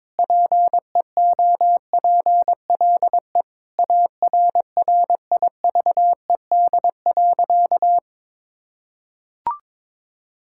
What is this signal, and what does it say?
Morse code, heard as a keyed tone, about 22 wpm, 700 Hz. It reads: PEOPLE ARRIVED.
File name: practice / 22wpm / Sentences with 3-Word Phrases - 22wpm